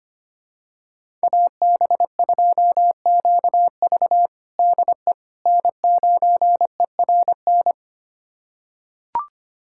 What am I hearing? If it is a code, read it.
Morse code, heard as a keyed tone, about 25 wpm, 700 Hz. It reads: AB2QV DE N9ERN
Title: USA Callsign DE Callsign - 5 Characters - 25wpm